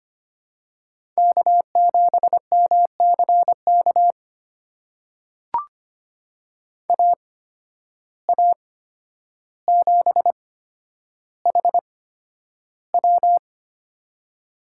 Morse code, heard as a keyed tone, about 25 wpm, spaced as for 11 wpm, 700 Hz. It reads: K7MCK AA7HW
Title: USA Call Signs - 5 Characters - 8x Character Spacing - 25wpm